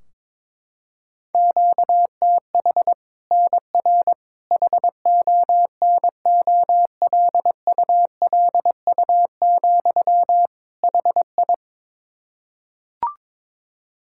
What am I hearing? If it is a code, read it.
Morse code, heard as a keyed tone, about 22 wpm, 700 Hz. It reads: QTH NR HONOLULU, HI
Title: Practice QSO - Exchange 1 - Beginning - 22wpm